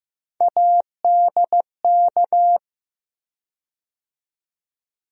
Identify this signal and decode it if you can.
Morse code, heard as a keyed tone, about 15 wpm, 700 Hz. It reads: ADK
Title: USA Call Sign Postfixes - 3 Letters 15wpm